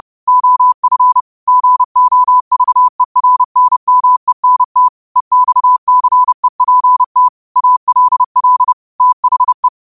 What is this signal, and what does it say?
Morse code, heard as a keyed tone, about 30 wpm, 1000 Hz. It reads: OF GOVERNMENT EXCEPT ALL THE
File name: qotdfiles / 30wpm